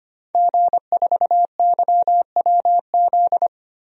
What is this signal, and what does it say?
Morse code, heard as a keyed tone, about 25 wpm, 700 Hz. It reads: G4YWZ
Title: DXCC-100 Call Signs - 5 Characters - 25wpm